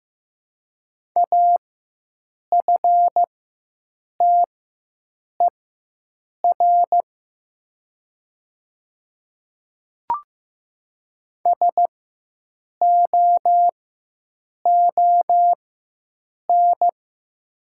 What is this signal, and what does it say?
Morse code, heard as a keyed tone, about 15 wpm, spaced as for 7 wpm, 700 Hz. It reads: AFTER SOON
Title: 4x Character Spacing - Top 500 Words - 15wpm